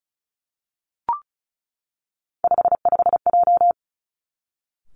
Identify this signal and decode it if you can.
Morse code, heard as a keyed tone, about 36 wpm, 700 Hz. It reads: 55J
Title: Three Challenging Letter-Number Combinations 35wpm